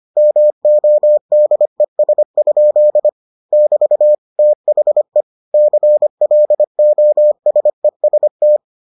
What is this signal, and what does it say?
Morse code, heard as a keyed tone, about 25 wpm, 600 Hz. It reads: MODES? <BT> THE CLOSEST